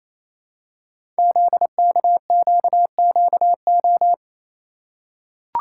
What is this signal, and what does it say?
Morse code, heard as a keyed tone, about 28 wpm, 700 Hz. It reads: ZKQQO
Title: Code Groups - 5 Letters - 28wpm